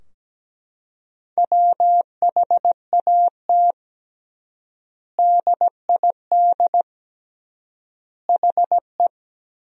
Morse code, heard as a keyed tone, about 17 wpm, 700 Hz. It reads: WHAT DID HE
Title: 3x Word-Spacing - Sentences from Top 300 Words - 17wpm